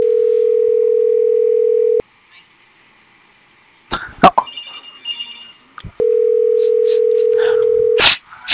Ringback tone (North America), 440 + 480 Hz.